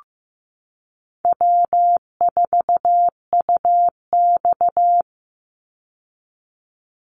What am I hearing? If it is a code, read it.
Morse code, heard as a keyed tone, about 15 wpm, 700 Hz. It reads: W4UX
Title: Easy Call Signs Repeated 3x - 15wpm